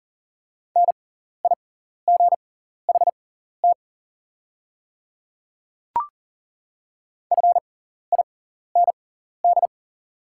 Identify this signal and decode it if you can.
Morse code, heard as a keyed tone, about 40 wpm, spaced as for 14 wpm, 690 Hz. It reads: NIGHT FIND
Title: 6x Character Spacing - Top 500 Words - 40wpm